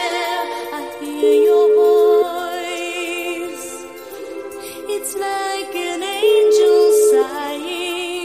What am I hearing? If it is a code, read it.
Ringback tone (Europe, 425 Hz), with noise.